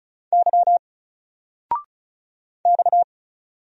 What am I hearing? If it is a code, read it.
Morse code, heard as a keyed tone, about 34 wpm, 700 Hz. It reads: Y X
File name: Single-Letter-Number-35wpm